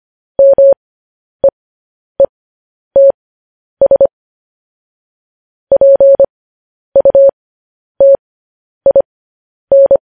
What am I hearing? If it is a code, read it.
Morse code, heard as a keyed tone, about 25 wpm, spaced as for 10 wpm, 550 Hz. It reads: MEETS PUTIN
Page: news headlines in morse code at 10 wpm